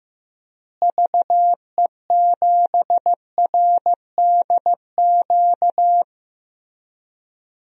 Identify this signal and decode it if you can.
Morse code, heard as a keyed tone, about 15 wpm, 700 Hz. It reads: VE7RDQ
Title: DXCC-100 Call Signs - 6 Characters - Repeated 3x - 15wpm